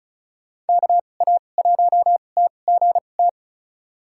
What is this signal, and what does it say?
Morse code, heard as a keyed tone, about 36 wpm, spaced as for 25 wpm, 700 Hz. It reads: KA1TGT
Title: USA Call Signs - 6 Characters - 2x Character Spacing - 35wpm